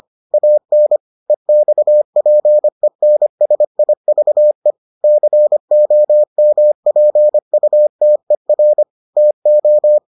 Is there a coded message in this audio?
Yes: Morse code from 0.3 to 10.0 s, heard as a keyed tone, about 25 wpm, 600 Hz. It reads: AN EXPENSIVE COMPUTER TO